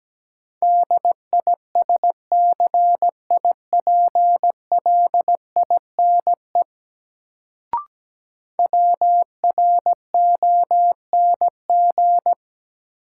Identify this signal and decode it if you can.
Morse code, heard as a keyed tone, about 17 wpm, 700 Hz. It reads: DISCIPLINE WRONG
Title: Sentences from Top 900 Words - New Word Drill - 17wpm